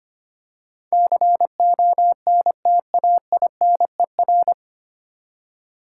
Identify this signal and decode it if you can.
Morse code, heard as a keyed tone, about 25 wpm, 700 Hz. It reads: CONTAINER
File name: practice / 25wpm / 9 Letter Words - First 3 Letters Given - 25wpm